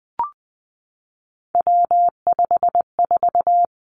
Morse code, heard as a keyed tone, about 20 wpm, 700 Hz. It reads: W54
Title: Three Challenging Letter-Number Combinations 20wpm